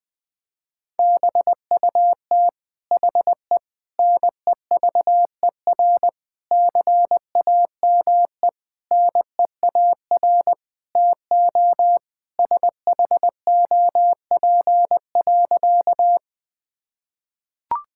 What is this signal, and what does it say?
Morse code, heard as a keyed tone, about 20 wpm, 700 Hz. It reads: BUT HE NEVER CAME NEAR TO SHOP.
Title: Sentences from Top 1600 Words 20wpm